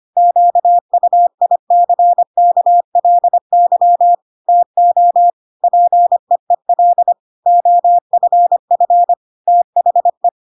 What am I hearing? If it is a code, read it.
Morse code, heard as a keyed tone, about 25 wpm, 700 Hz. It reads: QUICKLY TO PEEL OFF THE
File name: Pinocchio / Pinocchio ch1 25 25 ik1hge